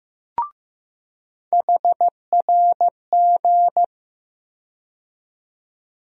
Morse code, heard as a keyed tone, about 15 wpm, 700 Hz. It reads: HRG
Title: USA Call Sign Postfixes - 3 Letters 15wpm